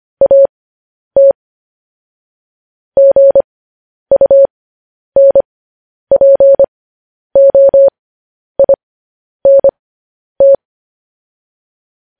Morse code, heard as a keyed tone, about 25 wpm, spaced as for 10 wpm, 550 Hz. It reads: AT GUNPOINT
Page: news headlines in morse code at 10 wpm